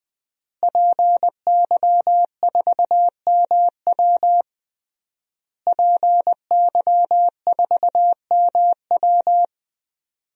Morse code, heard as a keyed tone, about 20 wpm, 700 Hz. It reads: PY4MW PY4MW
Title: DXCC-100 Call Signs - 5 Characters - Repeated 3x - 20wpm